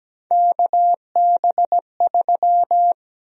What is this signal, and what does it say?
Morse code, heard as a keyed tone, about 17 wpm, 700 Hz. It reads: KB3
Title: USA Call Sign Prefixes - Repeated 2x - 17wpm